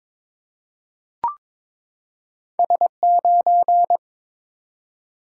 Morse code, heard as a keyed tone, about 22 wpm, 700 Hz. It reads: S9
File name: practice / 22wpm / Two Letter-Number Combinations - 22wpm